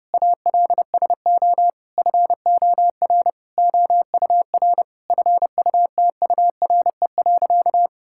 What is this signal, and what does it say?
Morse code, heard as a keyed tone, about 30 wpm, 700 Hz. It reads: ALSO FOR OUR FUTURE.
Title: Sentences from Top 1000 Words - Review Part 6 - 30wpm